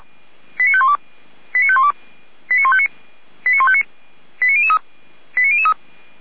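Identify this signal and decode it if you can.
Five-tone selective call (ZVEI), heard as quick runs of five tones, tones of about 70 ms. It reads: 89512 89512 89159 89159 89003 89003
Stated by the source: ZVEI five-tone call received on SDR, channel 455.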